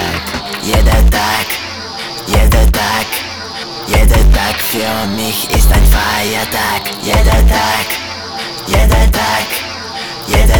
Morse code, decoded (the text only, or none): none